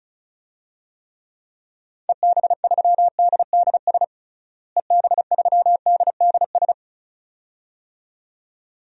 Morse code, heard as a keyed tone, about 36 wpm, 700 Hz. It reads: EB3DDS EB3DDS
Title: DXCC-100 Call Signs - 6 Characters - Repeated 2x - 35wpm